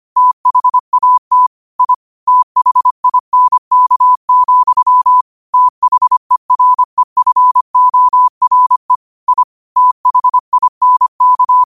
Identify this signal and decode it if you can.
Morse code, heard as a keyed tone, about 25 wpm, 1000 Hz. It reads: THAT I THINK, THEREFORE I THINK
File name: qotdfiles / 25wpm